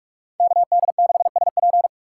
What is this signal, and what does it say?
Morse code, heard as a keyed tone, about 57 wpm, 700 Hz. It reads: KD6SP